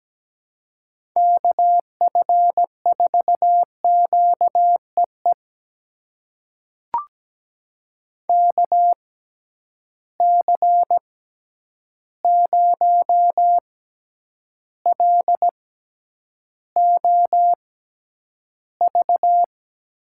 Morse code, heard as a keyed tone, about 17 wpm, spaced as for 9 wpm, 700 Hz. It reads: KF4QEE KC0LOV
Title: USA Call Signs - 6 Characters - 6x Character Spacing - 17wpm